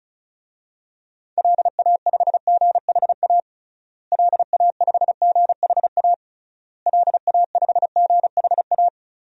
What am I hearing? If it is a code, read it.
Morse code, heard as a keyed tone, about 36 wpm, 700 Hz. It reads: LA5GHA LA5GHA LA5GHA
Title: DXCC-100 Call Signs - 6 Characters - Repeated 3x - 35wpm